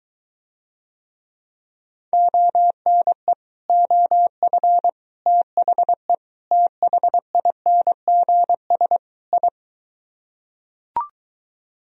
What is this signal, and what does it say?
Morse code, heard as a keyed tone, about 23 wpm, 700 Hz. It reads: ONE OF THE THINGS I